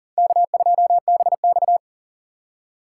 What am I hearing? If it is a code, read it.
Morse code, heard as a keyed tone, about 51 wpm, 700 Hz. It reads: K2BX